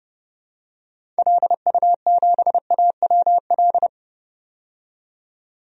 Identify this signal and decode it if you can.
Morse code, heard as a keyed tone, about 32 wpm, 700 Hz. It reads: LU7AWL
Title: DXCC-100 Call Signs - 6 Characters - 30wpm